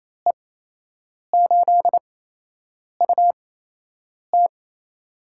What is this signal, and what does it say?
Morse code, heard as a keyed tone, about 28 wpm, spaced as for 8 wpm, 700 Hz. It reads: E8UT
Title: USA Call Signs - 5 Characters - 8x Character Spacing - 28wpm